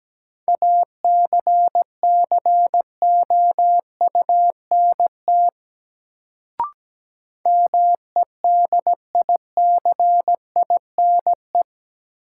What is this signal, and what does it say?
Morse code, heard as a keyed tone, about 17 wpm, 700 Hz. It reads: ACCOUNT MEDICINE
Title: Sentences from Top 900 Words - New Word Drill - 17wpm